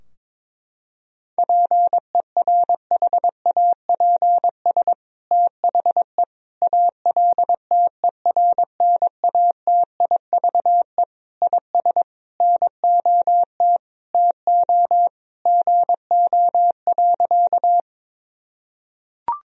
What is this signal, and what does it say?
Morse code, heard as a keyed tone, about 22 wpm, 700 Hz. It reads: PERHAPS THE ALTERNATIVE IS NOT TO GO.
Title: Sentences from Top 1000 Words - Review Part 4 - 22wpm